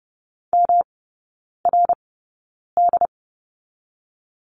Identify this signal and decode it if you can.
Morse code, heard as a keyed tone, about 30 wpm, 700 Hz. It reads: M R D
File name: Lesson 19e - Letter w - Three Characters 30wpm